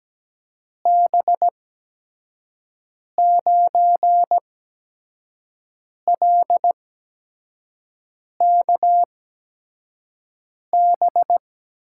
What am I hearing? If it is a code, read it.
Morse code, heard as a keyed tone, about 17 wpm, spaced as for 5 wpm, 700 Hz. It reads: B9LKB
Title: USA Call Signs - 6 Characters - 8x Character Spacing - 17wpm